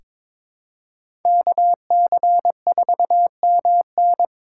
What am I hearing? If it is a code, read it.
Morse code, heard as a keyed tone, about 22 wpm, 700 Hz. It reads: KC4MN
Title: USA Call Signs - 8x Character Spacing - 22wpm